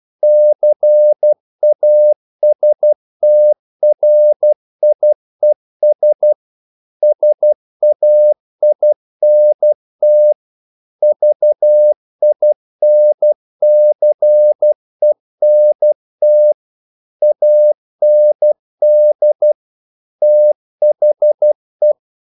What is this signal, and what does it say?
Morse code, heard as a keyed tone, about 12 wpm, 600 Hz. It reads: CASTRIES SAINT VINCENT AND THE
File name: CW / Practice / NationalCapitals12